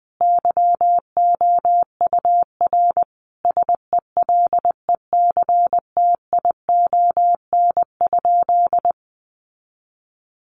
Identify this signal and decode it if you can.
Morse code, heard as a keyed tone, about 20 wpm, 700 Hz. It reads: YOUR SELECTION?
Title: Sentences from Top 1400 Words 20wpm